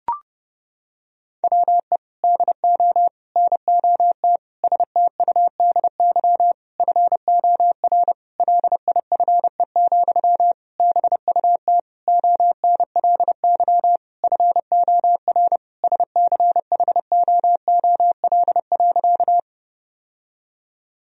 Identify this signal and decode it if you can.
Morse code, heard as a keyed tone, about 30 wpm, 700 Hz. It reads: WE DO NOT STUDY FOR LIFE, BUT ONLY FOR SCHOOL.
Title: Sentences from Top 1000 Words - Review Part 1 - 30wpm